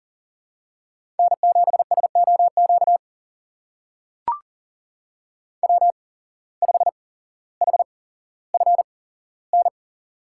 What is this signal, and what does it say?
Morse code, heard as a keyed tone, about 40 wpm, spaced as for 17 wpm, 700 Hz. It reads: N7SOQ W5HFN
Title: USA Call Signs - 8x Character Spacing - 40wpm